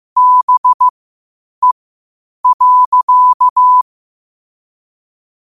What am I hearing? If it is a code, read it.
Morse code, heard as a keyed tone, about 15 wpm, spaced as for 8 wpm, 1000 Hz. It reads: BE.
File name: qotdfiles / 8wpm